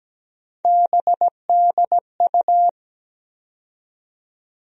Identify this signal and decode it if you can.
Morse code, heard as a keyed tone, about 17 wpm, 700 Hz. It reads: BDU
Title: USA Call Sign Postfixes - 3 Letters 17wpm